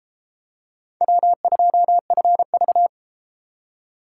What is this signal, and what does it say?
Morse code, heard as a keyed tone, about 32 wpm, 700 Hz. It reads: W2FV